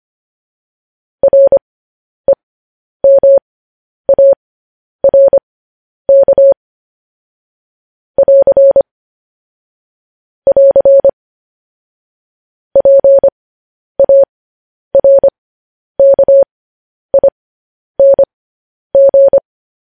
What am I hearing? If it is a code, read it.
Morse code, heard as a keyed tone, about 25 wpm, spaced as for 10 wpm, 550 Hz. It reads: REMARK <AR> <AR> PARKING
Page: news-headlines-in-morse-code-at-10-wpm